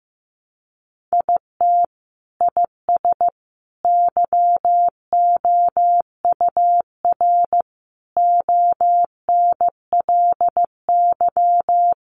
Morse code, heard as a keyed tone, about 15 wpm, 700 Hz. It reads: IT IS YOUR ONLY
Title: Sentences from Top 800 Words 15wpm